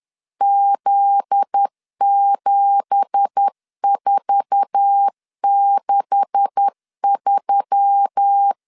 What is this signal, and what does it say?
Morse code, heard as a keyed tone, about 10 wpm, 800 Hz. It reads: Z7463
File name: LEZIONI / 9 2867 10wpm 2733